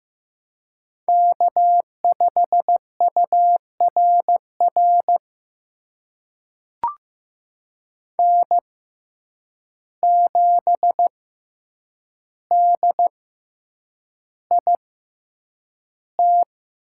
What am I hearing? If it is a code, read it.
Morse code, heard as a keyed tone, about 15 wpm, spaced as for 8 wpm, 700 Hz. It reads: K5URR N7DIT
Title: USA Call Signs - 5 Characters - 6x Character Spacing - 15wpm